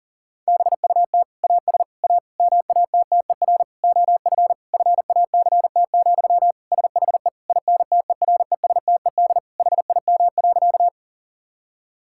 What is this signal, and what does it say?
Morse code, heard as a keyed tone, about 40 wpm, 700 Hz. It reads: BUT AS A MATTER OF FACT, SHE INTERESTED HIM.
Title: Sentences from Top 1000 Words - Review Part 6 - 40wpm